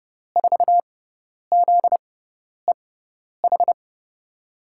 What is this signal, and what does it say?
Morse code, heard as a keyed tone, about 30 wpm, spaced as for 10 wpm, 700 Hz. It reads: 4ZEH
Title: USA Call Signs - 6 Characters - 6x Character Spacing - 30wpm